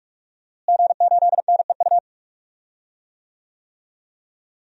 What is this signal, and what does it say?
Morse code, heard as a keyed tone, about 46 wpm, 700 Hz. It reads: G8NEU
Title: DXCC-100 Call Signs - 5 Characters - Repeated 2x - 45wpm